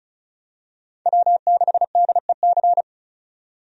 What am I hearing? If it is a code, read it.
Morse code, heard as a keyed tone, about 36 wpm, 700 Hz. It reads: W6DEC
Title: USA Call Signs - 5 Characters - 35wpm